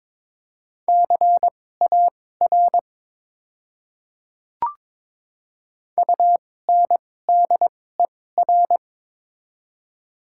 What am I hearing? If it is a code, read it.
Morse code, heard as a keyed tone, about 22 wpm, spaced as for 16 wpm, 700 Hz. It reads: CAR UNDER
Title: 2x Character Spacing - Top 300 Words - 22wpm